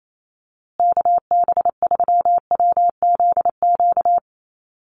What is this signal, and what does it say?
Morse code, heard as a keyed tone, about 28 wpm, 700 Hz. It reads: KB3WZQ
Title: USA Call Signs Repeated 3x 28wpm